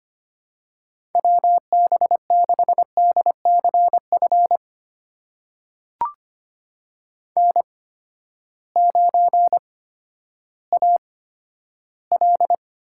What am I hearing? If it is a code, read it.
Morse code, heard as a keyed tone, about 25 wpm, spaced as for 12 wpm, 700 Hz. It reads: WB6DCF N9AL
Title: USA Call Signs - 8x Character Spacing - 25wpm